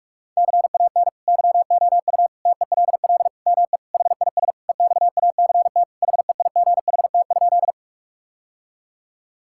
Morse code, heard as a keyed tone, about 46 wpm, 700 Hz. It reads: CAN YOU TELL ME HIS EXACT HEIGHT?
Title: Sentences from Top 1600 Words 45wpm